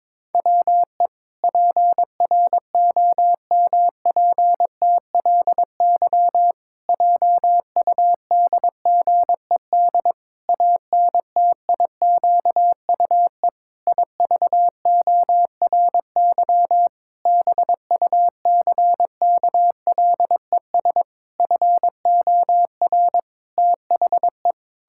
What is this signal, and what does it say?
Morse code, heard as a keyed tone, about 22 wpm, 700 Hz. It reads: WE PROMPTLY JUDGED ANTIQUE IVORY BUCKLES FOR THE
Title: Pangram Challenge v2 - 22wpm